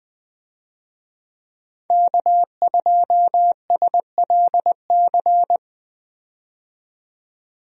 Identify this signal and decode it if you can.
Morse code, heard as a keyed tone, about 20 wpm, 700 Hz. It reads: K2SLC
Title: USA Call Signs - 5 Characters - 20wpm